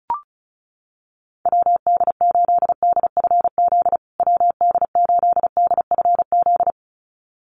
Morse code, heard as a keyed tone, about 34 wpm, 700 Hz. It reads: WD8DFZ WD8DFZ
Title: USA Call Signs Repeated 2x 35wpm